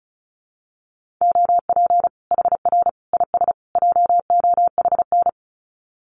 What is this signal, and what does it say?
Morse code, heard as a keyed tone, about 34 wpm, 700 Hz. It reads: OP HR IS JOHN
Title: Common QSO Phrases 35wpm